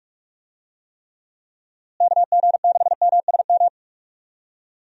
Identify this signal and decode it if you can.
Morse code, heard as a keyed tone, about 46 wpm, 700 Hz. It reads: KG6MSM
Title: USA Call Signs - 6 Characters - Repeated 3x - 45wpm